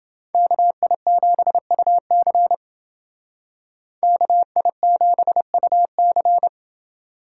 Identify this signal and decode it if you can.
Morse code, heard as a keyed tone, about 28 wpm, 700 Hz. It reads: KI7UC KI7UC